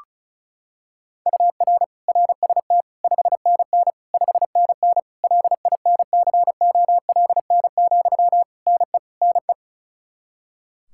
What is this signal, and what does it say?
Morse code, heard as a keyed tone, about 36 wpm, 700 Hz. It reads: UR RST 5NN 5NN LINCOLN, NE NE
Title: Practice QSO - Exchange 1 - Beginning - 35wpm